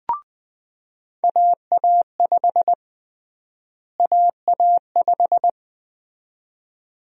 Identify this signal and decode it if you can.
Morse code, heard as a keyed tone, about 20 wpm, 700 Hz. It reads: AA5 AA5
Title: USA Call Sign Prefixes - Repeated 2x - 20wpm